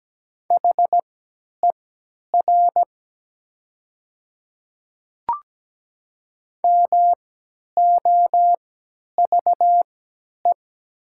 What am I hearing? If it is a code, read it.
Morse code, heard as a keyed tone, about 17 wpm, spaced as for 10 wpm, 700 Hz. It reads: HER MOVE
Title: 3x Character Spacing - Top 200 Words - 17wpm